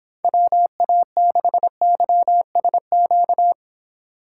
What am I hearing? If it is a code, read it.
Morse code, heard as a keyed tone, about 29 wpm, 700 Hz. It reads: WA6YSQ